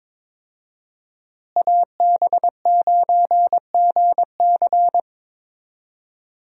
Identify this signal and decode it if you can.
Morse code, heard as a keyed tone, about 22 wpm, 700 Hz. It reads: AB9GC
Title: USA Call Signs - 5 Characters - Repeated 2x - 22wpm